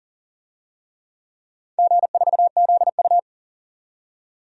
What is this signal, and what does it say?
Morse code, heard as a keyed tone, about 40 wpm, 700 Hz. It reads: G4ZU
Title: DXCC-100 Call Signs - 4 Characters - 40wpm